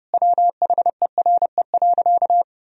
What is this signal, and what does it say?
Morse code, heard as a keyed tone, about 30 wpm, 700 Hz. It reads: WHERE.
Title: Sentences from Top 1000 Words - Review Part 1 - 30wpm